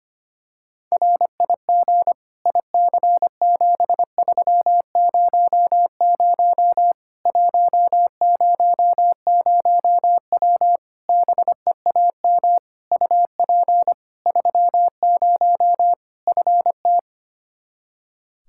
Morse code, heard as a keyed tone, about 25 wpm, 700 Hz. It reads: RIG IC7300 100W BEAM UP 30 FT
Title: Practice QSO - Exchange 2 - Info - 25wpm